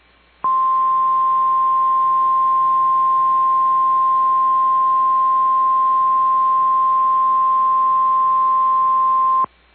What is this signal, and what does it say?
Emergency Alert System (SAME) tones (the 1050 Hz attention tone). It is heard from 0.4 to 9.5 s.
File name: EAS Logs / KMNR